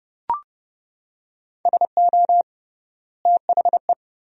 Morse code, heard as a keyed tone, about 31 wpm, 700 Hz. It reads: SO THE
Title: 3x Word-Spacing - Sentences from Top 400 Words - 30wpm